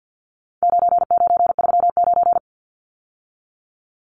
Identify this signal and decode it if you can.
Morse code, heard as a keyed tone, about 50 wpm, 700 Hz. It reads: 9939
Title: Numbers - 4-Digit 50wpm